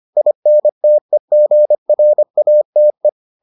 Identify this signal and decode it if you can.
Morse code, heard as a keyed tone, about 25 wpm, 600 Hz. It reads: INTEGRATE